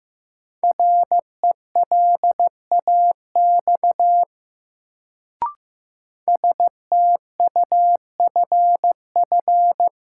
Morse code, heard as a keyed tone, about 15 wpm, 700 Hz. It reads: RELAX STUFF
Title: Sentences from Top 1400 Words - New Word Drill - 15wpm